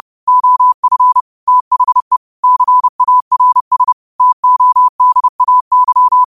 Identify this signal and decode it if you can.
Morse code, heard as a keyed tone, about 30 wpm, 1000 Hz. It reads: OF THE CARS TODAY
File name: qotdfiles / 30wpm